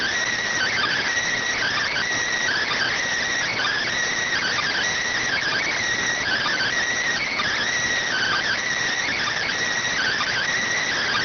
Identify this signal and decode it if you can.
Emergency Alert System (SAME) tones (header bursts), heard as harsh buzzing bursts. It is heard from 0.5 to 10.5 s.